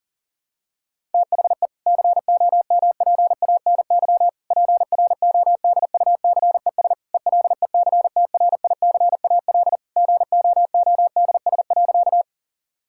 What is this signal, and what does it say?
Morse code, heard as a keyed tone, about 40 wpm, 700 Hz. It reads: THE COMPANY PRODUCES ELECTRICAL GOODS.
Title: Sentences from Top 1000 Words - Review Part 6 - 40wpm